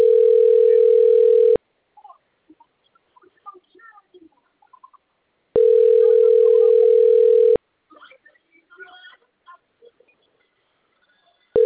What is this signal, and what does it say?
Ringback tone (North America), 440 + 480 Hz.